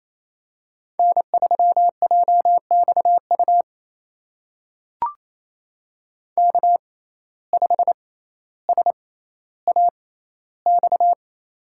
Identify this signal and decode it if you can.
Morse code, heard as a keyed tone, about 28 wpm, spaced as for 14 wpm, 700 Hz. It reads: N3JXU K5SAX
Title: USA Call Signs - 5 Characters - 6x Character Spacing - 28wpm